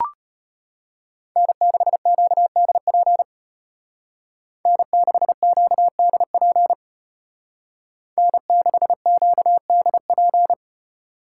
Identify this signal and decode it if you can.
Morse code, heard as a keyed tone, about 33 wpm, 700 Hz. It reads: N6QDP N6QDP N6QDP